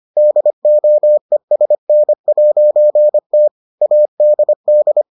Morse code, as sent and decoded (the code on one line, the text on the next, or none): -.. --- . ... -. .----. - / .- -.. -..
DOESN'T ADD